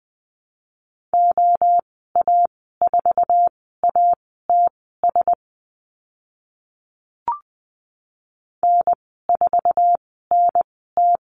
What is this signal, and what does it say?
Morse code, heard as a keyed tone, about 20 wpm, spaced as for 15 wpm, 700 Hz. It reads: OA4ATS N4NT
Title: 069 Lesson 12g - Letter d - Call Signs 20wpm